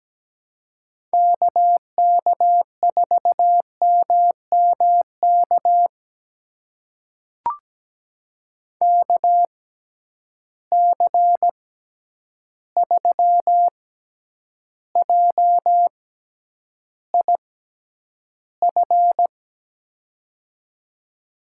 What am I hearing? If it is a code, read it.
Morse code, heard as a keyed tone, about 17 wpm, spaced as for 9 wpm, 700 Hz. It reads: KK4MMK KC3JIF
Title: USA Call Signs - 6 Characters - 6x Character Spacing - 17wpm